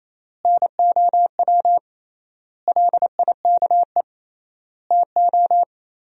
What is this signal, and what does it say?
Morse code, heard as a keyed tone, about 28 wpm, 700 Hz. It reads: NOW LIKE TO
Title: 3x Word-Spacing - Sentences from Top 200 Words - 28wpm